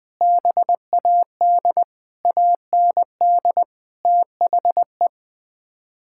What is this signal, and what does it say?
Morse code, heard as a keyed tone, about 20 wpm, 700 Hz. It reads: BAD AND THE
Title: Sets of 6 Words - 20wpm